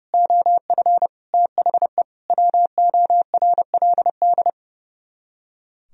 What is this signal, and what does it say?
Morse code, heard as a keyed tone, about 39 wpm, 700 Hz. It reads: OF THE WORLD